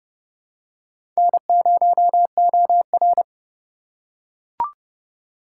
Morse code, heard as a keyed tone, about 31 wpm, 700 Hz. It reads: N0OR
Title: USA Call Signs - 4 Characters - 3x Character Spacing - 30wpm